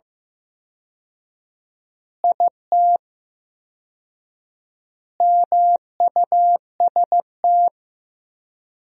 Morse code, heard as a keyed tone, about 16 wpm, 700 Hz. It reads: IT MUST
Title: 4x Word-Spacing - Sentences from Top 500 Words - 15wpm